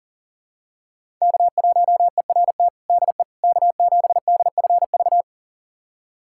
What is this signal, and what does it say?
Morse code, heard as a keyed tone, about 40 wpm, 700 Hz. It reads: K1ERT DE K7DFV
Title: USA Callsign DE Callsign - 5 Characters - 40wpm